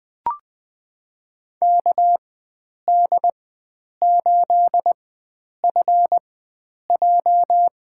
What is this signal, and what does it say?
Morse code, heard as a keyed tone, about 20 wpm, spaced as for 9 wpm, 700 Hz. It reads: KD8FJ
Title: USA Call Signs - 4x Character Spacing - 20wpm